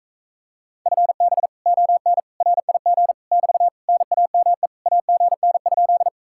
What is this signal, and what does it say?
Morse code, heard as a keyed tone, about 45 wpm, 700 Hz. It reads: FB ON RIG <BT> NAME AGN?